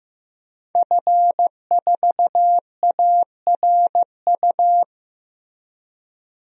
Morse code, heard as a keyed tone, about 15 wpm, 700 Hz. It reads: F4ARU
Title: DXCC-100 Call Signs - 5 Characters - Repeated 3x - 15wpm